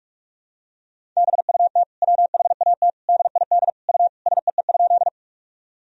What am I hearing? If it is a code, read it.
Morse code, heard as a keyed tone, about 46 wpm, 700 Hz. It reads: BUT WHAT DID U SEE?
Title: Sentences from Top 100 Words and Abbreviations 45wpm